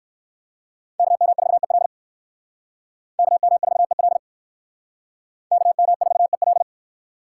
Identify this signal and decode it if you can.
Morse code, heard as a keyed tone, about 57 wpm, 700 Hz. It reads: KM4EL KM4EL KM4EL